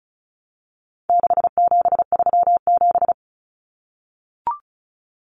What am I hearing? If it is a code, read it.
Morse code, heard as a keyed tone, about 34 wpm, 700 Hz. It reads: 6737
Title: Common Cut Numbers - 4-Digit 35wpm